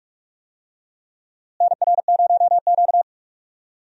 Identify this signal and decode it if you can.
Morse code, heard as a keyed tone, about 45 wpm, 700 Hz. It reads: NR0Q
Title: USA Call Signs - 4 Characters - Repeated 2x - 45wpm